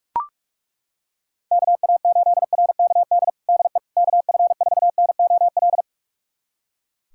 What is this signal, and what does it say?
Morse code, heard as a keyed tone, about 42 wpm, 700 Hz. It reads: KA8RKD DE KF4NOL
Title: USA Callsign DE Callsign - 6 Characters - 45wpm